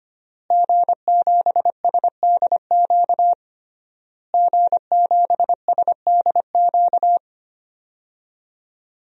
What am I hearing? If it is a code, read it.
Morse code, heard as a keyed tone, about 25 wpm, 700 Hz. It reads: G7SDQ G7SDQ
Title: DXCC-100 Call Signs - 5 Characters - Repeated 3x - 25wpm